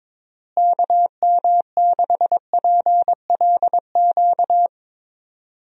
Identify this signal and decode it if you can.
Morse code, heard as a keyed tone, about 22 wpm, 700 Hz. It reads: KM6PLQ
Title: USA Call Signs - 6 Characters - Repeated 3x - 22wpm